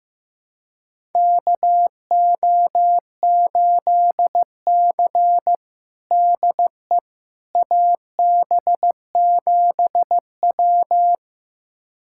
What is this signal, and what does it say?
Morse code, heard as a keyed tone, about 15 wpm, 700 Hz. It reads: KO8C DE AB7W
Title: USA Callsign DE Callsign - 4 Characters - 15wpm